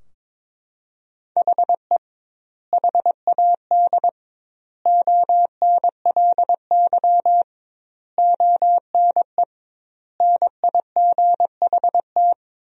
Morse code, heard as a keyed tone, about 22 wpm, 700 Hz. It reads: HE HAD ONLY ONE NIGHT
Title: 2x Word-Spacing - Sentences from Top 300 Words - 22wpm